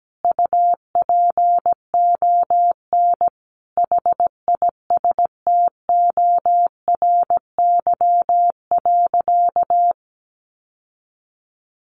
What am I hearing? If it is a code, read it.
Morse code, heard as a keyed tone, about 17 wpm, 700 Hz. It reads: UPON HISTORY.
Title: Sentences from Top 1100 Words 17wpm